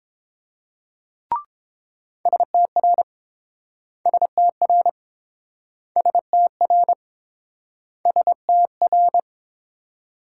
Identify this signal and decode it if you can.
Morse code, heard as a keyed tone, about 30 wpm, 700 Hz. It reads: STR STR STR STR